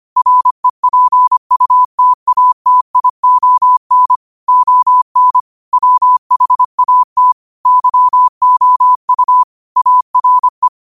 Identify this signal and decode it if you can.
Morse code, heard as a keyed tone, about 25 wpm, 1000 Hz. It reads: REPUTATION ON WHAT YOU ARE
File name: qotdfiles / 25wpm